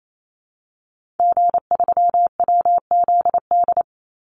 Morse code, heard as a keyed tone, about 28 wpm, 700 Hz. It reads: G3WZD
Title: Call Signs Repeated 2x - 28wpm